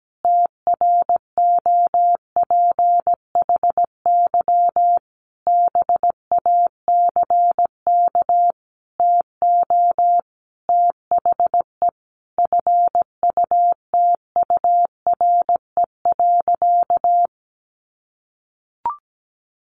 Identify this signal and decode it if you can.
Morse code, heard as a keyed tone, about 17 wpm, 700 Hz. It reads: TROPHY BACK TO THE FUTURE.
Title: A Sound Of Thunder 17wpm